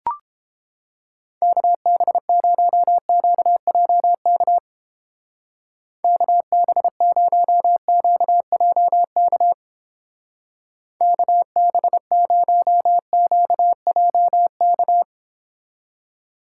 Morse code, heard as a keyed tone, about 30 wpm, 700 Hz. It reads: KB0QJK KB0QJK KB0QJK